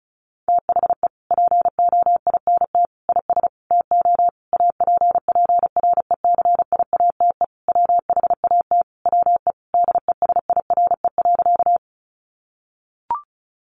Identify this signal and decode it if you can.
Morse code, heard as a keyed tone, about 34 wpm, 700 Hz. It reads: THE POINT IS TO APPRECIATE WHAT WE DESIRE.
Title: Sentences from Top 1300 Words 35wpm